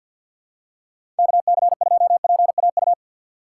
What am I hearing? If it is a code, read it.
Morse code, heard as a keyed tone, about 48 wpm, 700 Hz. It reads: KC2PAU
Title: USA Call Signs - 6 Characters - 4x Character Spacing - 50wpm